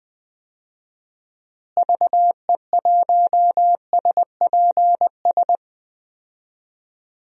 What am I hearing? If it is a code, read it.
Morse code, heard as a keyed tone, about 20 wpm, 700 Hz. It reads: VE1SPS
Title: DXCC-100 Call Signs - 6 Characters - Repeated 2x - 20wpm